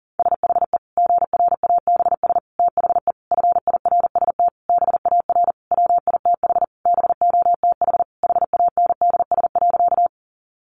Morse code, heard as a keyed tone, about 40 wpm, 700 Hz. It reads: SHE GRABS THE FIRST BAR WITH BOTH HANDS.
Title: Easy Phrases 40wpm